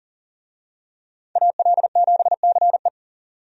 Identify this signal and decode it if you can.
Morse code, heard as a keyed tone, about 40 wpm, 700 Hz. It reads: AL7CE
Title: USA Call Signs - 5 Characters - 40wpm